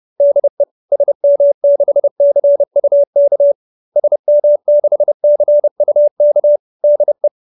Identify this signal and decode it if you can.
Morse code, heard as a keyed tone, about 30 wpm, 590 Hz. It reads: DE SM6CUK SM6CUK DE
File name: QSO319 30